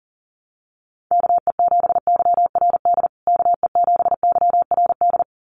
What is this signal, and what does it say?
Morse code, heard as a keyed tone, about 40 wpm, 700 Hz. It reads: KE7YRD KE7YRD
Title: USA Call Signs Repeated 2x 40wpm